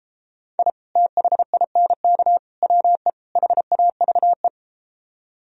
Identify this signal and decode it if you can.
Morse code, heard as a keyed tone, about 33 wpm, 700 Hz. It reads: I THINK WE HAVE